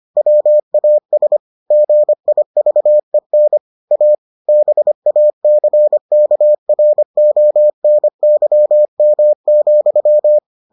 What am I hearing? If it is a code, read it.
Morse code, heard as a keyed tone, about 25 wpm, 600 Hz. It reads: WAS GIVEN A BACKRONYM,